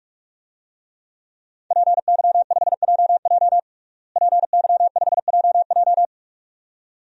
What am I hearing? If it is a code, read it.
Morse code, heard as a keyed tone, about 46 wpm, 700 Hz. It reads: PY5JJ PY5JJ
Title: DXCC-100 Call Signs - 5 Characters - Repeated 2x - 45wpm